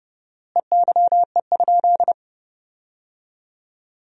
Morse code, heard as a keyed tone, about 30 wpm, 700 Hz. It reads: EYE?
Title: 4x Word-Spacing - Sentences from Top 500 Words - 30wpm